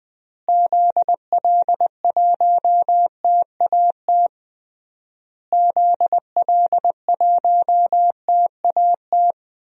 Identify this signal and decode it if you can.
Morse code, heard as a keyed tone, about 21 wpm, 700 Hz. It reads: ZL1TAT ZL1TAT
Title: DXCC-100 Call Signs - 6 Characters - Repeated 3x - 20wpm